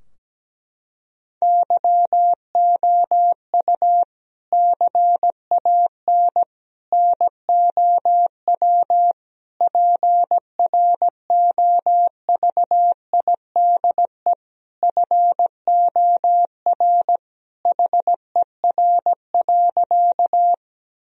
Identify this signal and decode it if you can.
Morse code, heard as a keyed tone, about 17 wpm, 700 Hz. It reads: YOU CAN NOW PROVIDE FOR HER.
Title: Sentences from Top 1000 Words - Review Part 3 - 17wpm